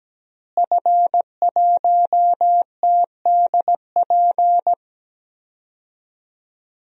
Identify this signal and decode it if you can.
Morse code, heard as a keyed tone, about 17 wpm, 700 Hz. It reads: F1TDP
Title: DXCC-100 Call Signs - 5 Characters - Repeated 2x - 17wpm